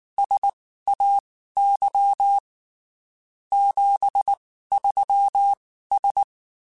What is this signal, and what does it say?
Morse code, heard as a keyed tone, about 19 wpm, spaced as for 14 wpm, 800 Hz. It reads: SAY 73S
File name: (spaced) / QSOs Spaced-mp3s / WA7QOX 2 @18 sp